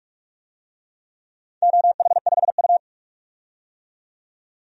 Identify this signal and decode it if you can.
Morse code, heard as a keyed tone, about 44 wpm, 700 Hz. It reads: OH5U
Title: DXCC-100 Call Signs - 4 Characters - 45wpm